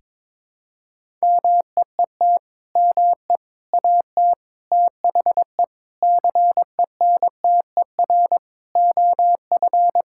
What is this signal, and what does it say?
Morse code, heard as a keyed tone, about 22 wpm, 700 Hz. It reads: MEET ME AT THE CENTER OF
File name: practice / 22wpm / Sentences with 3-Word Phrases - 22wpm